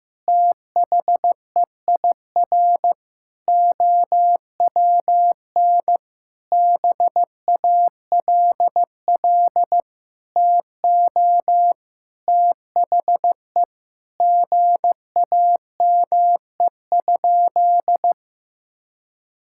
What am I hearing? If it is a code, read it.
Morse code, heard as a keyed tone, about 15 wpm, 700 Hz. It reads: THEIR OWN BALL TO THE GAME?
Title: Sentences from Top 1000 Words - Review Part 6 - 15wpm